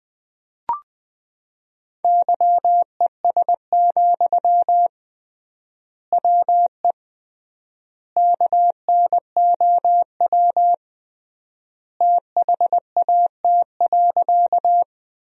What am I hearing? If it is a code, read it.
Morse code, heard as a keyed tone, about 21 wpm, 700 Hz. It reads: YES, WE KNOW THAT.
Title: 3x Word-Spacing - Sentences from Top 500 Words - 20wpm